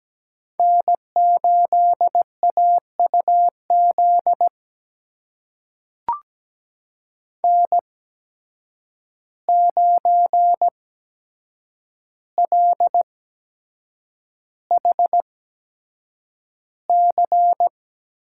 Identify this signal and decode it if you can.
Morse code, heard as a keyed tone, about 17 wpm, spaced as for 7 wpm, 700 Hz. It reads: N8AUZ N9LHC
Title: USA Call Signs - 5 Characters - 8x Character Spacing - 17wpm